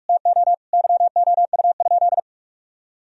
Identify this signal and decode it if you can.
Morse code, heard as a keyed tone, about 46 wpm, 700 Hz. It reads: TO YOU?
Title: Sentences from Top 1000 Words - Review Part 4 - 45wpm